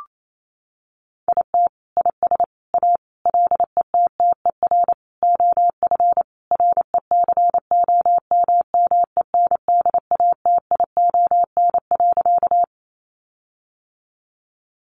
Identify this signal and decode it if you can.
Morse code, heard as a keyed tone, about 28 wpm, 700 Hz. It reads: IT IS A LETTER OF RECOMMENDATION.
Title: Sentences from Top 1400 Words 28wpm